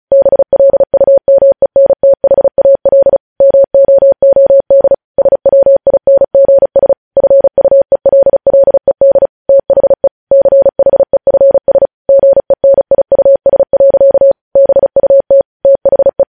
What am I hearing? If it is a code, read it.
Morse code, heard as a keyed tone, about 36 wpm, 550 Hz. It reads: BLUMENTHAL MOOD SWINGS FUELLED THE CHEFS GENIUS. BUT THE
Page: news headlines in morse code at 35 wpm